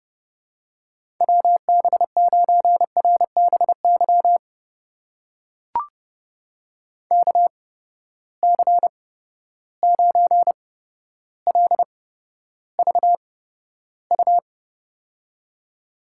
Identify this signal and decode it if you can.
Morse code, heard as a keyed tone, about 30 wpm, spaced as for 13 wpm, 700 Hz. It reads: WB9RBY KC9LVU
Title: USA Call Signs - 6 Characters - 8x Character Spacing - 30wpm